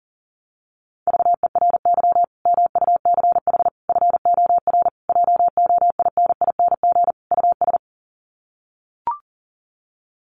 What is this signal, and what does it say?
Morse code, heard as a keyed tone, about 40 wpm, 700 Hz. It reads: VERY MUCH FOR JOINING US
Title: Sets of 5 Words 40wpm